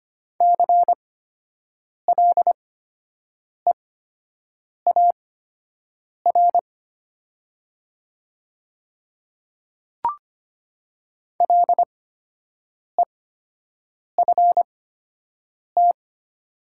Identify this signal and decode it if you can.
Morse code, heard as a keyed tone, about 25 wpm, spaced as for 7 wpm, 700 Hz. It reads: CLEAR LEFT
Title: 8x Character Spacing - Top 500 Words - 25wpm